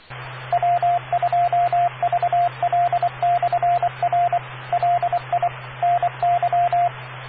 Morse code, heard as a keyed tone, about 24 wpm, 700 Hz, with radio noise. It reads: W2VL/R LI NY